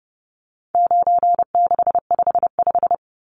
Morse code, heard as a keyed tone, about 30 wpm, 700 Hz. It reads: 9655
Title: Numbers - 4-Digit 30wpm